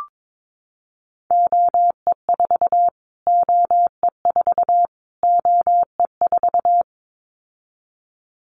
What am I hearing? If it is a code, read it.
Morse code, heard as a keyed tone, about 22 wpm, 700 Hz. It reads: OE4 OE4 OE4
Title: Callsign Prefixes for Top 25 DXCC - Repeated 3x - 22wpm